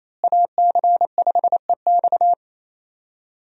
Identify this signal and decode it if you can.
Morse code, heard as a keyed tone, about 28 wpm, 700 Hz. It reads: AC5EX
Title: USA Call Signs - 5 Characters - Repeated 2x - 28wpm